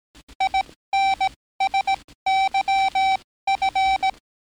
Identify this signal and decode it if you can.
Morse code, heard as a keyed tone, about 18 wpm, spaced as for 15 wpm, 770 Hz. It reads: INSYF
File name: training / NAVPERS 11049to11060 / NAVPERS 11056 RA RB